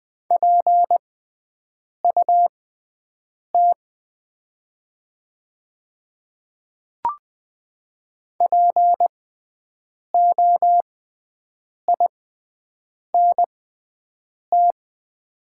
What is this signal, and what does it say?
Morse code, heard as a keyed tone, about 20 wpm, spaced as for 7 wpm, 700 Hz. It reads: PUT POINT
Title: 6x Character Spacing - Top 500 Words - 20wpm